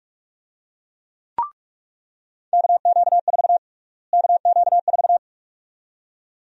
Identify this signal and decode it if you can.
Morse code, heard as a keyed tone, about 46 wpm, 700 Hz. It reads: KQ4 KQ4
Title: USA Call Sign Prefixes - Repeated 2x - 45wpm